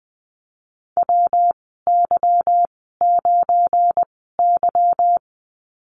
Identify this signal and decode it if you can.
Morse code, heard as a keyed tone, about 20 wpm, spaced as for 15 wpm, 700 Hz. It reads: WY9Y
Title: 193 Lesson 31g - Letter v - Call Signs 20wpm